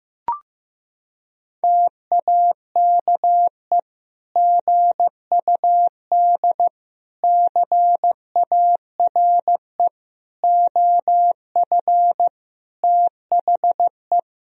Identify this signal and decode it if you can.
Morse code, heard as a keyed tone, about 16 wpm, 700 Hz. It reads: TAKE GUD CARE OF THE
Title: Sentences from Top 400 Words and Abbreviations 15wpm